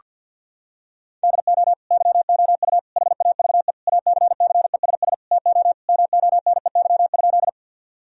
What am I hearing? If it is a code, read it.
Morse code, heard as a keyed tone, about 50 wpm, 700 Hz. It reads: DO YOU HAVE ACCESS TO MONEY?
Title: Sentences from Top 1000 Words - Review Part 4 - 50wpm